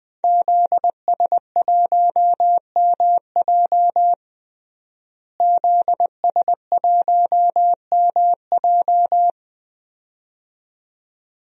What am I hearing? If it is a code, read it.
Morse code, heard as a keyed tone, about 20 wpm, 700 Hz. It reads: ZS1MJ ZS1MJ
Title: DXCC-100 Call Signs - 5 Characters - Repeated 2x - 20wpm